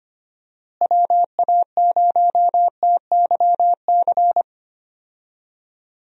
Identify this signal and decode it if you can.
Morse code, heard as a keyed tone, about 25 wpm, 700 Hz. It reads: WA0TYC
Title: USA Call Signs - 6 Characters - Repeated 3x - 25wpm